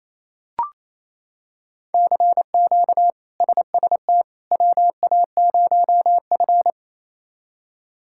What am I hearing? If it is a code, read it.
Morse code, heard as a keyed tone, about 27 wpm, 700 Hz. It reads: CQ SST WA0F
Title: K1USN SST QSO Practice - 28wpm